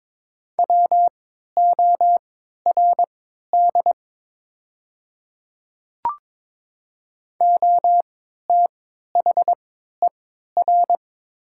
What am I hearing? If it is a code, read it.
Morse code, heard as a keyed tone, about 22 wpm, spaced as for 12 wpm, 700 Hz. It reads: WORD OTHER
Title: 3x Character Spacing - Top 100 Words - 22wpm